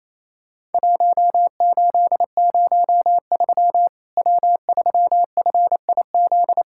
Morse code, heard as a keyed tone, about 28 wpm, 700 Hz. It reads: 1803 W3FIZ
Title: Worked All Europe DX Contest - QTC Exchange - 28wpm